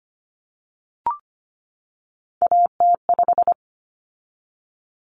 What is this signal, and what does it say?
Morse code, heard as a keyed tone, about 25 wpm, 700 Hz. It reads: AT5
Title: Common Cut Numbers - 3-Digit 25wpm